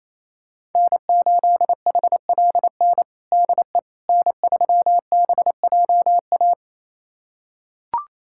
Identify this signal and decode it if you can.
Morse code, heard as a keyed tone, about 28 wpm, 700 Hz. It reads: N8HLN DE N3BJA
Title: USA Callsign DE Callsign - 5 Characters - 28wpm